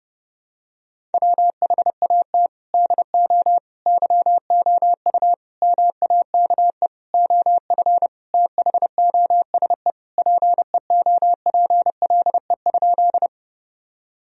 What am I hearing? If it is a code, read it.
Morse code, heard as a keyed tone, about 30 wpm, 700 Hz. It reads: WHAT DO YOU MAKE OF THOSE PEOPLE?
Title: Sentences from Top 1000 Words - Review Part 1 - 30wpm